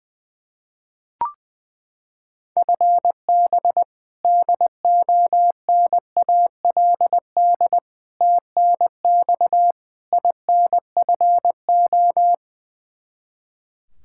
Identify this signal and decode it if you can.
Morse code, heard as a keyed tone, about 20 wpm, 700 Hz. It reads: FB DONALD TNX INFO
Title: Practice QSO - Exchange 3 - More Info - 20wpm